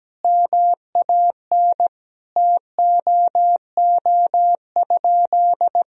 Morse code, heard as a keyed tone, about 17 wpm, 700 Hz. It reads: MAN TOO?
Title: Sentences from Top 1000 Words - Review Part 4 - 17wpm